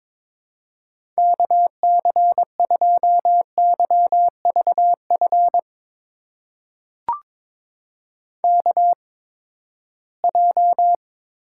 Morse code, heard as a keyed tone, about 22 wpm, spaced as for 15 wpm, 700 Hz. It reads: KC2YVF KJ
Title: USA Call Signs - 6 Characters - 8x Character Spacing - 22wpm